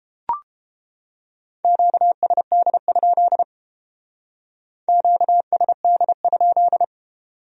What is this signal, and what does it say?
Morse code, heard as a keyed tone, about 30 wpm, 700 Hz. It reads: QSD? QSD?